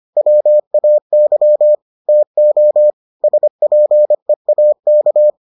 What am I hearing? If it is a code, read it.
Morse code, heard as a keyed tone, about 25 wpm, 600 Hz. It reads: WAY TO SPEAK